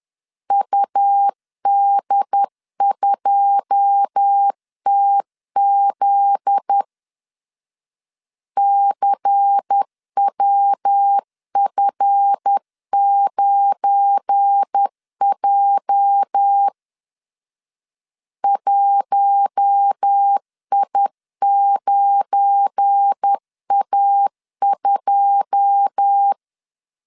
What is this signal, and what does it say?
Morse code, heard as a keyed tone, about 10 wpm, 800 Hz. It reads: UD2TZ CWF9J 1I9A2
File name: LEZIONI / 9 2 10wpm 2736